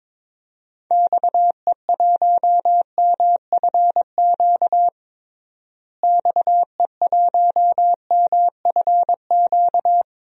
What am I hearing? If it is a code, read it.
Morse code, heard as a keyed tone, about 22 wpm, 700 Hz. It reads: XE1MFQ XE1MFQ
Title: DXCC-100 Call Signs - 6 Characters - Repeated 2x - 22wpm